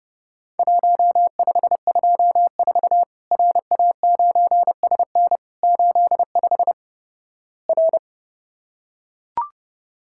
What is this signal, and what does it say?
Morse code, heard as a keyed tone, about 32 wpm, 700 Hz. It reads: 1524 RA9SN 85 R
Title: Worked All Europe DX Contest - QTC Exchange - 30wpm